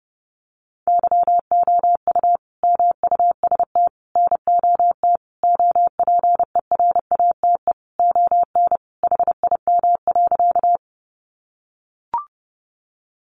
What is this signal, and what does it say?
Morse code, heard as a keyed tone, about 30 wpm, 700 Hz. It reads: YOU MUST NOT OPERATE ON HIM.
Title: Sentences from Top 1500 Words 30wpm